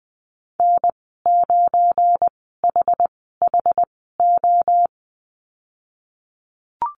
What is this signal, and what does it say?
Morse code, heard as a keyed tone, about 20 wpm, spaced as for 15 wpm, 700 Hz. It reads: N9HHO
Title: 193 Lesson 31g - Letter v - Call Signs 20wpm